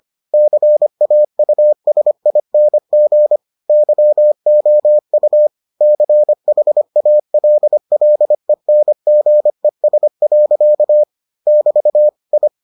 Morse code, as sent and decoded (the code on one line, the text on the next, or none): -.-. .- ..- ... .. -. --. / -.-- --- ..- / -.-. .... .- .-.. .-.. -. --. ... .-.-.- / -...- / ..
CAUSING YOU CHALLENGES. <BT> I